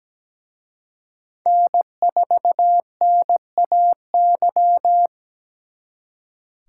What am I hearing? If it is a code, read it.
Morse code, heard as a keyed tone, about 20 wpm, 700 Hz. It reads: N4NAY
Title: Speed Racing - USA 1x3 Call Signs 17wpm